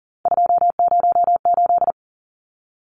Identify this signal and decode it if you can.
Morse code, heard as a keyed tone, about 40 wpm, 700 Hz. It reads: 208
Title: Numbers - 3-Digit 40wpm